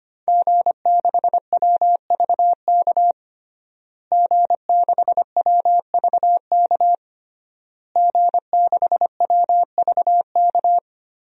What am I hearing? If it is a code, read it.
Morse code, heard as a keyed tone, about 25 wpm, 700 Hz. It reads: G6WVK G6WVK G6WVK
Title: DXCC-100 Call Signs - 5 Characters - Repeated 3x - 25wpm